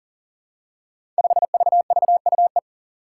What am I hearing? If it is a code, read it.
Morse code, heard as a keyed tone, about 39 wpm, 700 Hz. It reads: 5VVUE
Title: Five Challenging Letter-Number Combinations 40wpm